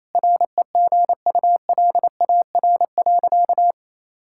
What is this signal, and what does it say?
Morse code, heard as a keyed tone, about 28 wpm, 700 Hz. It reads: REGULAR.
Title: Sentences from Top 1000 Words - Review Part 6 - 28wpm